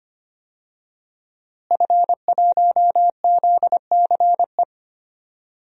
Morse code, heard as a keyed tone, about 25 wpm, 700 Hz. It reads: F1ZCE
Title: DXCC-100 Call Signs - 5 Characters - Repeated 2x - 25wpm